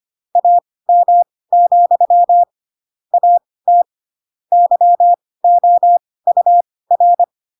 Morse code, heard as a keyed tone, about 25 wpm, spaced as for 18 wpm, 700 Hz. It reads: AM, AT YOUR
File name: Pinocchio / Pinocchio ch2 12 25 ik1hge